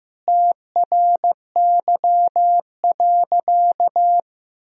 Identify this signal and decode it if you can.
Morse code, heard as a keyed tone, about 15 wpm, 700 Hz. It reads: TRY.
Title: Sentences from Top 200 Words and Abbreviations 15wpm